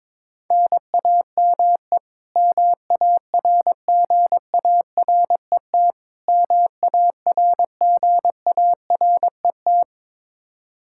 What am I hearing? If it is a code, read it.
Morse code, heard as a keyed tone, about 22 wpm, 700 Hz. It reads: NAME MARGARET MARGARET
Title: Practice QSO - Exchange 1 - Beginning - 22wpm